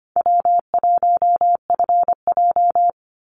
Morse code, heard as a keyed tone, about 25 wpm, 700 Hz. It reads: W1FJ
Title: Easy Call Signs Repeated 2x - 25wpm